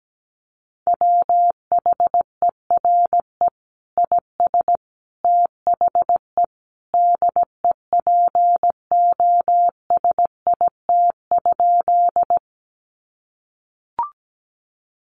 Morse code, heard as a keyed tone, about 17 wpm, 700 Hz. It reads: WHERE IS THE DEPOSIT?
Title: Sentences from Top 1300 Words 17wpm